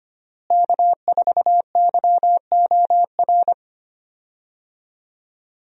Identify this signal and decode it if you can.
Morse code, heard as a keyed tone, about 25 wpm, 700 Hz. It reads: K4YOR
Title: USA Call Signs - 5 Characters - Repeated 2x - 25wpm